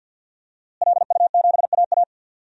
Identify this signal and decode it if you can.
Morse code, heard as a keyed tone, about 51 wpm, 700 Hz. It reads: LU7AA
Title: DXCC-100 Call Signs - 5 Characters - Repeated 3x - 50wpm